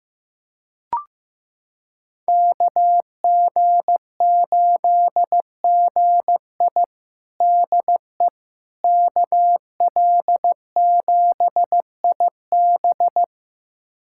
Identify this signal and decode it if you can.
Morse code, heard as a keyed tone, about 15 wpm, 700 Hz. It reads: KG8GI DE KL7IB
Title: USA Callsign DE Callsign 15wpm